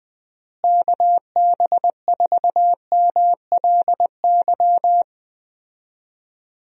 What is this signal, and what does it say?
Morse code, heard as a keyed tone, about 20 wpm, 700 Hz. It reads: KB4MLY
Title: USA Call Signs - 6 Characters - Repeated 2x - 20wpm